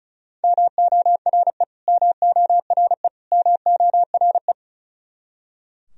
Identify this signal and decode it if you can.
Morse code, heard as a keyed tone, about 36 wpm, 700 Hz. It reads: MORE MORE MORE
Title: Top 100 Words Repeated 35wpm